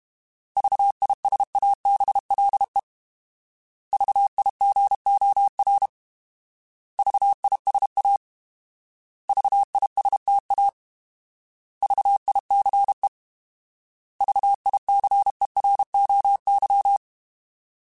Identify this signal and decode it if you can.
Morse code, heard as a keyed tone, about 32 wpm, 800 Hz. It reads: VISABLE VIGOR VISA VISTA VICE VICEROY
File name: Alphabetical / Alphabetical mp3s / Vs @30